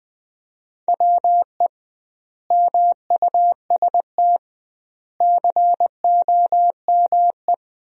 Morse code, heard as a keyed tone, about 20 wpm, 700 Hz. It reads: WE MUST COME
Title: 2x Word-Spacing - Sentences from Top 200 Words - 20wpm